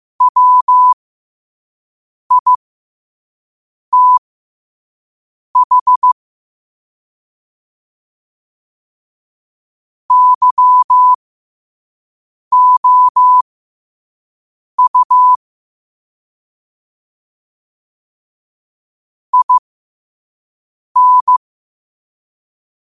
Morse code, heard as a keyed tone, about 15 wpm, spaced as for 5 wpm, 1000 Hz. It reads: WITH YOU IN
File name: qotdfiles / 5wpm